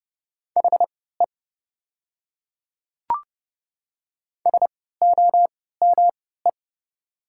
Morse code, heard as a keyed tone, about 30 wpm, spaced as for 17 wpm, 700 Hz. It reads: HE SOME
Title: 3x Character Spacing - Top 100 Words - 30wpm